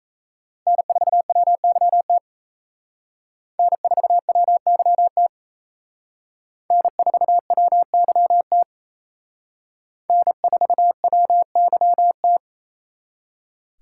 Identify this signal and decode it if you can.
Morse code, heard as a keyed tone, about 36 wpm, 700 Hz. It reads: N4WYT N4WYT N4WYT N4WYT